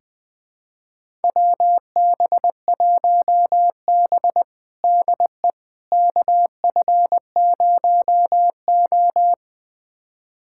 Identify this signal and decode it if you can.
Morse code, heard as a keyed tone, about 20 wpm, 700 Hz. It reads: WB1B DE KF0O
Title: USA Callsign DE Callsign - 4 Characters - 20wpm